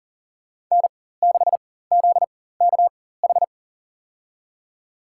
Morse code, heard as a keyed tone, about 40 wpm, spaced as for 19 wpm, 700 Hz. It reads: N6ZKH
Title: USA Call Signs - 5 Characters - 4x Character Spacing - 40wpm